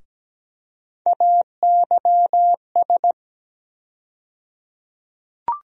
Morse code, heard as a keyed tone, about 17 wpm, 700 Hz. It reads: AYS
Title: USA Call Sign Postfixes - 3 Letters 17wpm